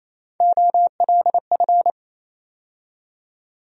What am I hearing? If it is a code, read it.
Morse code, heard as a keyed tone, about 28 wpm, 700 Hz. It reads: OLF
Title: USA Call Sign Postfixes - 3 Letters 28wpm